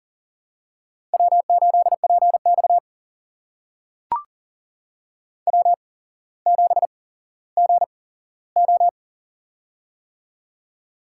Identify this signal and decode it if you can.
Morse code, heard as a keyed tone, about 40 wpm, spaced as for 17 wpm, 700 Hz. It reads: W8PX W7GO
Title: USA Call Signs - 4 Characters - 8x Character Spacing - 40wpm